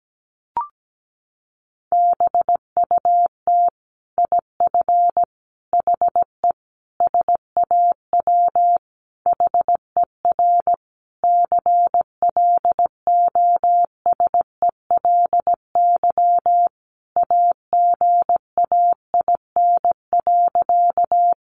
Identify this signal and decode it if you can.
Morse code, heard as a keyed tone, about 17 wpm, 700 Hz. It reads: BUT IF HE SAW HER CLOSELY AGAIN.
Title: Sentences from Top 1400 Words 17wpm